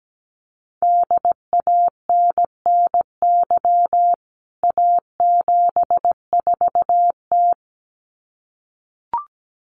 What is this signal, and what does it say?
Morse code, heard as a keyed tone, about 17 wpm, 700 Hz. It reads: DANNY A74T
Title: CWT Exchange 17wpm